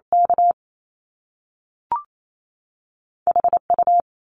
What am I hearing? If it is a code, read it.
Morse code, heard as a keyed tone, about 28 wpm, spaced as for 3 wpm, 700 Hz. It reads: KH U
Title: Top Two Letter Combinations in Words 28wpm